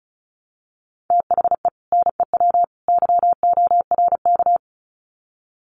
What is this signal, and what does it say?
Morse code, heard as a keyed tone, about 34 wpm, 700 Hz. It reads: THE NEW YORK
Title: Sets of 3 Words 35wpm